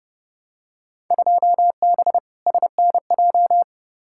Morse code, heard as a keyed tone, about 30 wpm, 700 Hz. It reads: 2B SNJ
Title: Field Day Exchanges Encore - 30wpm